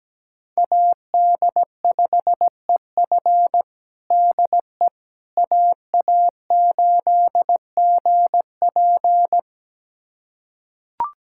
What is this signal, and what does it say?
Morse code, heard as a keyed tone, about 17 wpm, 700 Hz. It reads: AD5EF DE AA8GP
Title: USA Callsign DE Callsign - 5 Characters - 17wpm